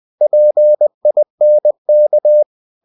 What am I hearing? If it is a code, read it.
Morse code, heard as a keyed tone, about 20 wpm, 600 Hz. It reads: PINK
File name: cw-books / winnie-the-pooh / winnie-20wpm-page-0047